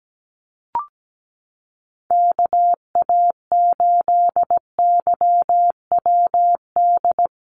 Morse code, heard as a keyed tone, about 17 wpm, 700 Hz. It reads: KA8YWD
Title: USA Call Signs Repeated 2x 17wpm